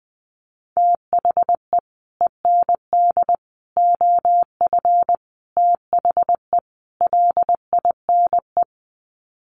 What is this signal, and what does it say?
Morse code, heard as a keyed tone, about 20 wpm, 700 Hz. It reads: THE END OF THE LINE
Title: Sets of 5 Words Encore2 20wpm